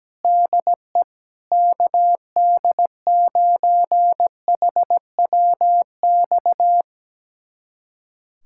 Morse code, heard as a keyed tone, about 17 wpm, 700 Hz. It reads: DE KD9HWX
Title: USA Callsign DE Callsign - 6 Characters - 17wpm